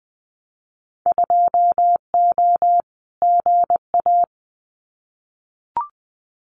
Morse code, heard as a keyed tone, about 19 wpm, 700 Hz. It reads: 2O GA
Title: Winter Field Day Exchanges Encore 20wpm